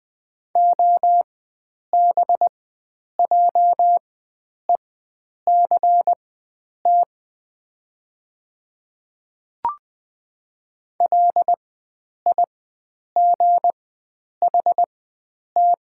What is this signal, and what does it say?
Morse code, heard as a keyed tone, about 20 wpm, spaced as for 9 wpm, 700 Hz. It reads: OBJECT LIGHT
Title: 4x Character Spacing - Top 500 Words - 20wpm